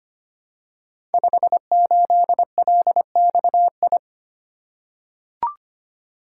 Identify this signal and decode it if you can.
Morse code, heard as a keyed tone, about 25 wpm, 700 Hz. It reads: 58LXI
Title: Code Groups - 5 Letter:Numbers - 25wpm